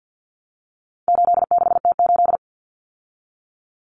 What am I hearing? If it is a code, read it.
Morse code, heard as a keyed tone, about 50 wpm, 700 Hz. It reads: 86T8
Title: Common Cut Numbers - 4-Digit 50wpm